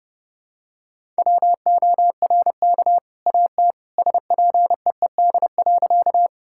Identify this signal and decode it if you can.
Morse code, heard as a keyed tone, about 30 wpm, 700 Hz. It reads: WORK AT SPEED.
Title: Sentences from Top 1000 Words - Review Part 5 - 30wpm